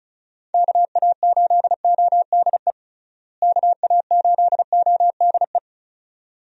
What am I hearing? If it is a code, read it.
Morse code, heard as a keyed tone, about 34 wpm, 700 Hz. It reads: KA8ODE KA8ODE
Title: USA Call Signs - 6 Characters - Repeated 2x - 35wpm